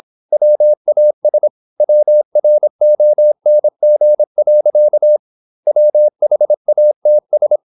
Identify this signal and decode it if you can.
Morse code, heard as a keyed tone, about 26 wpm, 600 Hz. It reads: WAS WRONG. WHATS
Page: Never 26wpm | CW med Gnister